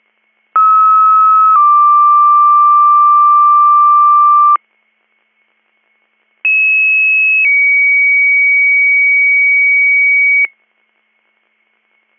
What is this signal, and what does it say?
Two-tone page, heard as a one-second tone, then a three-second tone, about 1290 then 1180 Hz, about 2600 then 2250 Hz. It is heard from 0.5 to 10.5 s.